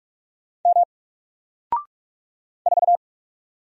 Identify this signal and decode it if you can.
Morse code, heard as a keyed tone, about 46 wpm, 700 Hz. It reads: M 4
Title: Single Letter-Number 45wpm